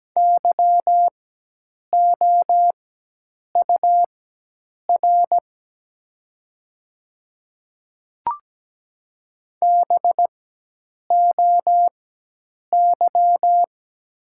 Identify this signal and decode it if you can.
Morse code, heard as a keyed tone, about 17 wpm, spaced as for 8 wpm, 700 Hz. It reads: YOUR BOY
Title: 4x Character Spacing - Top 300 Words - 17wpm